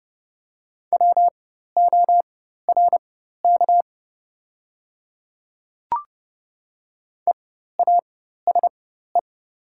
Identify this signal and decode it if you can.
Morse code, heard as a keyed tone, about 30 wpm, spaced as for 14 wpm, 700 Hz. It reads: WORK EASE
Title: 4x Character Spacing - Top 300 Words - 30wpm